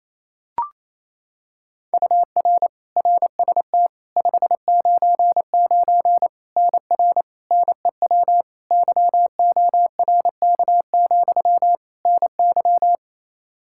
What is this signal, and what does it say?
Morse code, heard as a keyed tone, about 28 wpm, 700 Hz. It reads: UR RST 599 NR NEW YORK, NY
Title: K1USN SST QSO Practice 28wpm